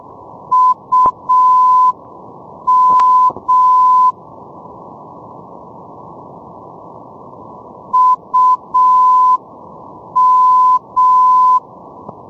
Morse code, heard as a keyed tone, about 6 wpm, 1010 Hz. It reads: UM UM
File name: ndb ivanovskoye um-405-net-moskva